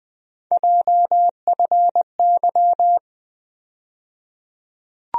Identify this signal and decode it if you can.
Morse code, heard as a keyed tone, about 20 wpm, 700 Hz. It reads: JFY
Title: USA Call Sign Postfixes - 3 Letters 20wpm